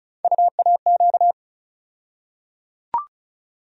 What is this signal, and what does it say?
Morse code, heard as a keyed tone, about 34 wpm, 700 Hz. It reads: UAQ
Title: USA Call Sign Postfixes - 3 Letters 35wpm